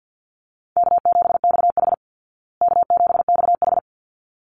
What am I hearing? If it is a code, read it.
Morse code, heard as a keyed tone, about 50 wpm, 700 Hz. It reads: K7XH K7XH
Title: Call Signs Repeated 3x - 50wpm